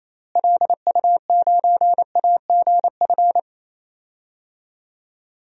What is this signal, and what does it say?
Morse code, heard as a keyed tone, about 27 wpm, 700 Hz. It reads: LU9AGF
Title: DXCC-100 Call Signs - 6 Characters - Repeated 2x - 28wpm